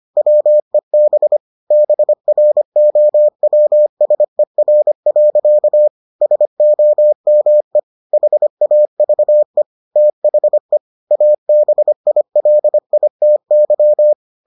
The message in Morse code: WEB BROWSER. SOME HAVE THE ABILITY

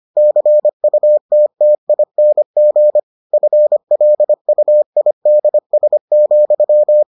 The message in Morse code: CUTTING FLUIDS,

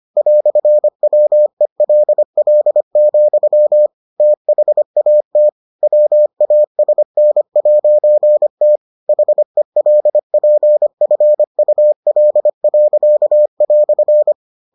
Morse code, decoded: "WELL, THAT WASN'T HELPFUL."

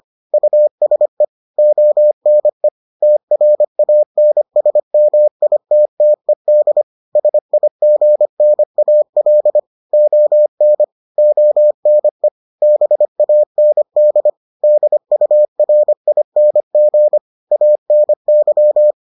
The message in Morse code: USE ONE TRANSMITTED SIGNAL ON ONE BAND DURING ANY

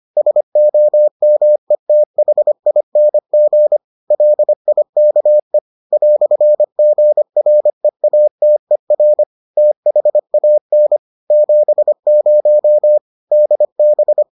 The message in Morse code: SOMETHING LIKE "GREATER THAN 70 DB

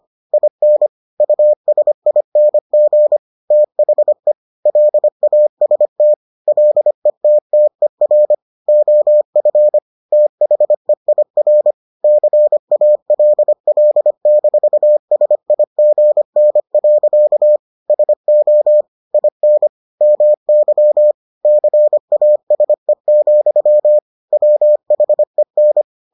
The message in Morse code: IN USING THE LAST LETTER OF THEIR CALL-SIGN. SO IN MY CASE, WHEN

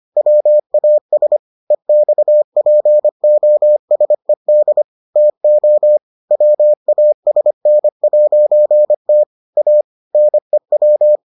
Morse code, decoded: WAS EXPOSED TO WASN'T A NEW

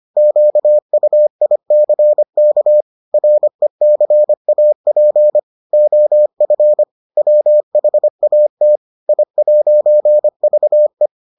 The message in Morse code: QUICK RECAP OF WHAT I'VE

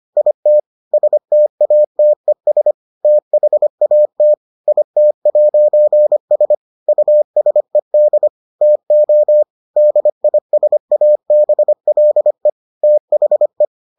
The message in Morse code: IT STATES THAT IT'S USED TO DISABLE THE

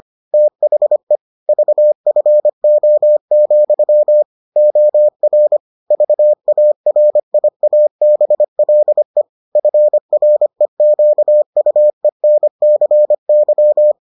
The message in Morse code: THE VFO, OR VARIABLE FREQUENCY

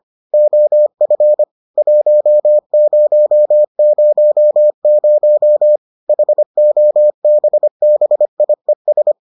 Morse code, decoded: OF 1000 HOBBIES

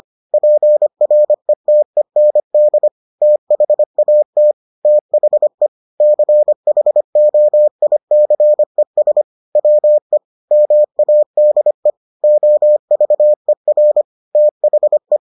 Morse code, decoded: PRETEND THAT THE CHOICES WE MADE OVER THE

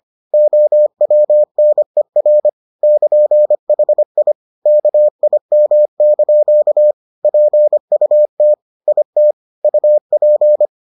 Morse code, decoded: OWNER <KN>HI KIM) PUT IT UP

